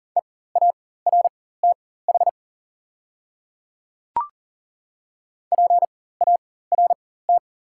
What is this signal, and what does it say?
Morse code, heard as a keyed tone, about 40 wpm, spaced as for 19 wpm, 700 Hz. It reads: EARTH PART
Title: 4x Character Spacing - Top 200 Words - 40wpm